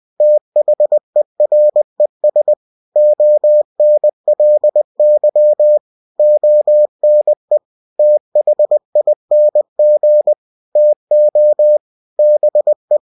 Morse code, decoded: THERES ONLY ONE THING TO BE

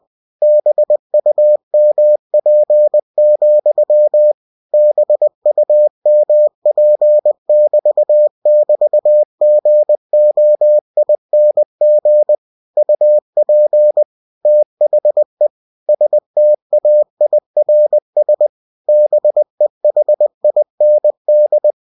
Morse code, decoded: BUMP, BUMP<BT><BT>GOING UP THE STAIRS BEHIND

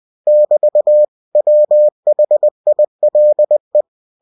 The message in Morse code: <BT> WHILE

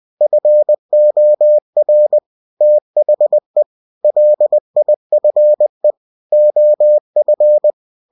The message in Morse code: FOR THE LIFE OF

